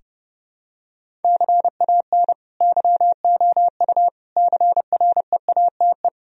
Morse code, heard as a keyed tone, about 30 wpm, 700 Hz. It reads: CAN YOU CREATE
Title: Sentences from Top 1000 Words - Review Part 3 - 30wpm